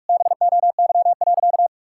Morse code, heard as a keyed tone, about 44 wpm, 700 Hz. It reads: BOY.
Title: Sentences from Top 1000 Words - Review Part 6 - 45wpm